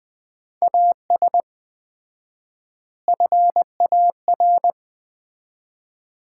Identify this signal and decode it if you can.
Morse code, heard as a keyed tone, about 20 wpm, 700 Hz. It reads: AS FAR
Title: 4x Word-Spacing - Sets of 4 Words - 20wpm